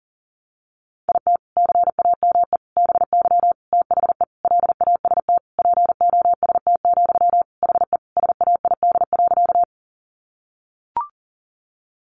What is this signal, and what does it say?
Morse code, heard as a keyed tone, about 40 wpm, 700 Hz. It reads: IT CAME BY THE LAST POST, HE SAID.
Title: Sentences from Top 1000 Words 40wpm